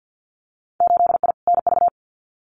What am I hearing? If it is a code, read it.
Morse code, heard as a keyed tone, about 50 wpm, 700 Hz. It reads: 8I NV
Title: Winter Field Day Exchanges Encore 50wpm